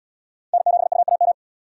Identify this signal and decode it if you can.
Morse code, heard as a keyed tone, about 77 wpm, 700 Hz. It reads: N7UAM